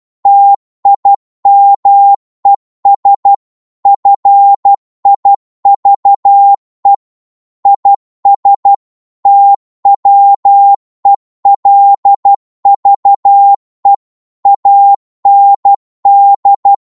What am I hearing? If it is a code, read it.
Morse code, heard as a keyed tone, about 12 wpm, 800 Hz. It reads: TIMES FIVE IS TWELVE AND